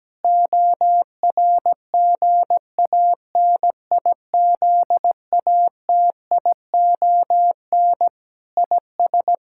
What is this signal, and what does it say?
Morse code, heard as a keyed tone, about 17 wpm, 700 Hz. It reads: ORGANIZATION IS
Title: Sentences from Top 1000 Words - Review Part 5 - 17wpm